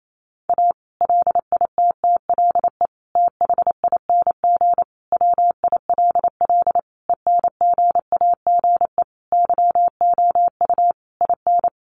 Morse code, heard as a keyed tone, about 28 wpm, 700 Hz. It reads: A LITTLE THING WILL ENGAGE YOU IN
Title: Sentences from Top 1500 Words 28wpm